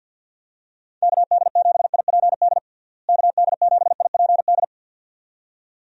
Morse code, heard as a keyed tone, about 51 wpm, 700 Hz. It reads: KD7IPD KD7IPD
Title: USA Call Signs - 6 Characters - Repeated 2x - 50wpm